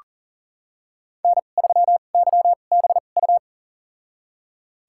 Morse code, heard as a keyed tone, about 40 wpm, spaced as for 29 wpm, 700 Hz. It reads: N3YBU
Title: USA Call Signs - 5 Characters - 2x Character Spacing - 40wpm